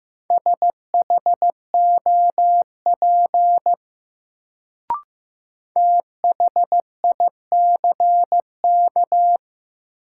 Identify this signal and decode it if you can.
Morse code, heard as a keyed tone, about 15 wpm, 700 Hz. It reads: SHOP THICK
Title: Sentences from Top 1600 Words - New Word Drill - 15wpm